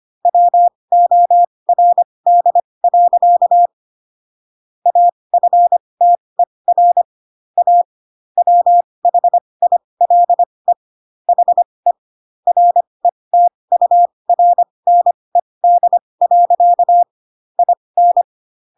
Morse code, heard as a keyed tone, about 25 wpm, spaced as for 20 wpm, 700 Hz. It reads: WORD. AFTER A WHILE HE RETURNED. IN
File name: Pinocchio / Pinocchio ch8 15 25 ik1hge